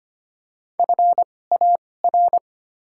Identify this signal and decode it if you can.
Morse code, heard as a keyed tone, about 25 wpm, spaced as for 18 wpm, 700 Hz. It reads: FAR
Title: 2x Character Spacing - Top 500 Words - 25wpm